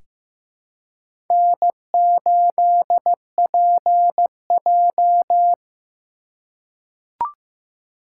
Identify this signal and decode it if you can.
Morse code, heard as a keyed tone, about 15 wpm, 700 Hz. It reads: N8PJ
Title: USA Call Signs - 4 Characters - 3x Character Spacing - 15wpm